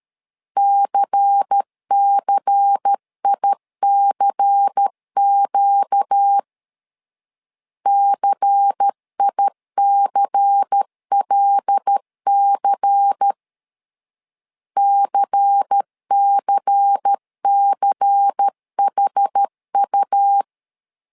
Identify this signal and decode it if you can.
Morse code, heard as a keyed tone, about 13 wpm, 800 Hz. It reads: CCICQ CICLC CCCHU
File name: LEZIONI / 5 LBVC / 5 LBVC 12wpm 1955